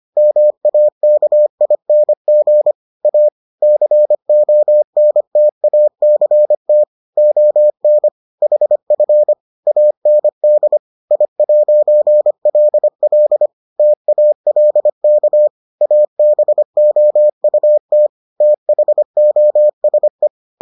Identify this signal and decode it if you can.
Morse code, heard as a keyed tone, about 25 wpm, 600 Hz. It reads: MAKING A CONTACT ON HF AND I'LL TALK ABOUT THOSE